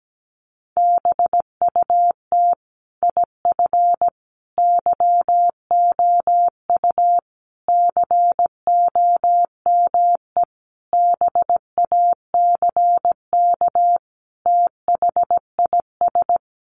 Morse code, heard as a keyed tone, about 17 wpm, 700 Hz. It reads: BUT IF YOU COME BACK THIS
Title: A Sound Of Thunder 17wpm